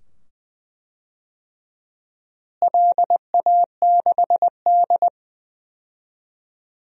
Morse code, heard as a keyed tone, about 20 wpm, 700 Hz. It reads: LA6D
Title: DXCC-100 Call Signs - 4 Characters - Repeated 2x - 20wpm